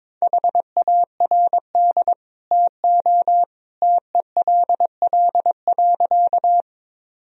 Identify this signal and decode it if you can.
Morse code, heard as a keyed tone, about 22 wpm, 700 Hz. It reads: HARD TO TELL.
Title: Sentences from Top 1000 Words - Review Part 1 - 22wpm